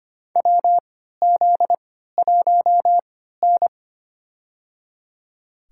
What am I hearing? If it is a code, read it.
Morse code, heard as a keyed tone, about 25 wpm, spaced as for 14 wpm, 700 Hz. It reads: WZ1N
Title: USA Call Signs - 4 Characters - 3x Character Spacing - 25wpm